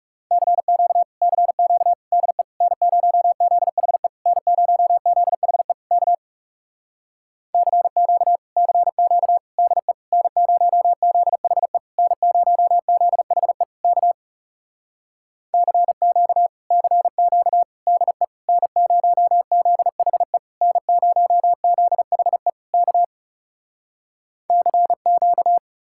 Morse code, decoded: CQ CQ DE N0ZHE N0ZHE K CQ CQ DE N0ZHE N0ZHE K CQ CQ DE N0ZHE N0ZHE K CQ

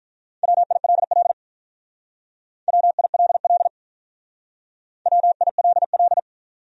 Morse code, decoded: WILL WILL WILL

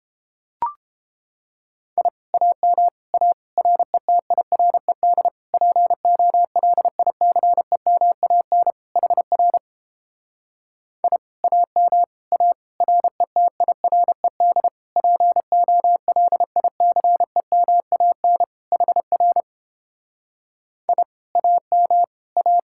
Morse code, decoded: I AM A RETIRED POLICEMAN HR I AM A RETIRED POLICEMAN HR I AM A

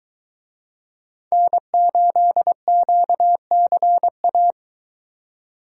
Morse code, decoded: N8QCA

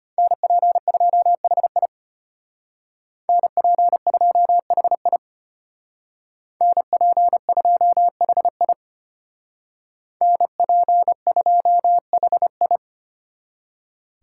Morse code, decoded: NP2HI NP2HI NP2HI NP2HI

